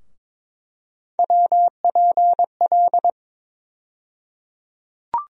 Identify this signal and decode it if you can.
Morse code, heard as a keyed tone, about 22 wpm, 700 Hz. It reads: WPL
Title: USA Call Sign Postfixes - 3 Letters 22wpm